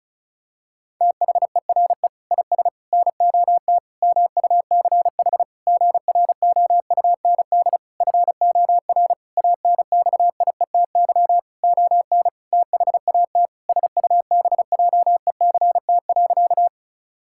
Morse code, heard as a keyed tone, about 36 wpm, 700 Hz. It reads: THERE IS NOT MUCH GROUND FOR ANXIETY ON THAT SUBJECT.
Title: Sentences from Top 1600 Words 35wpm